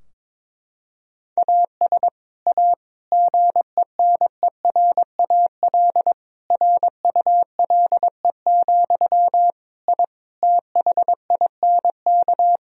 Morse code, heard as a keyed tone, about 22 wpm, 700 Hz. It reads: AS A GENERAL RULE, I THINK
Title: Sentences from Top 1000 Words - Review Part 3 - 22wpm